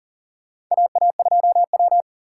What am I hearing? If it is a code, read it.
Morse code, heard as a keyed tone, about 40 wpm, 700 Hz. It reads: AA2W
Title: USA Call Signs - 4x Character Spacing - 40wpm